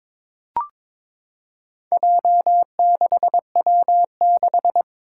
Morse code, heard as a keyed tone, about 22 wpm, 700 Hz. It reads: J6W6
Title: Four Challenging Letter-Number Combinations 22wpm